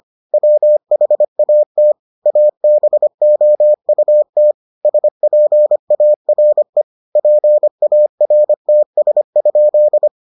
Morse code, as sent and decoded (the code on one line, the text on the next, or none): .-- .... .- - / .- -... --- ..- - / ... .--. .- .-. / .--. .- .-. - ... ..--..
WHAT ABOUT SPARE PARTS?